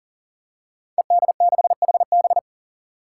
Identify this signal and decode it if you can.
Morse code, heard as a keyed tone, about 40 wpm, 700 Hz. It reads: ED6HB
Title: Five Challenging Letter-Number Combinations 40wpm